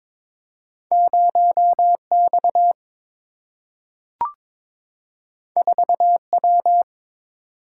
Morse code, heard as a keyed tone, about 22 wpm, 700 Hz. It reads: 0 X 4 W
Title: Single-Letter Single-Number 22wpm